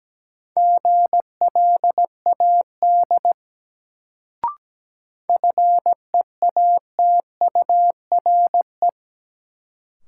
Morse code, heard as a keyed tone, about 17 wpm, 700 Hz. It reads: GLAD FEATURE
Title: Sentences from Top 800 Words - New Word Drill - 17wpm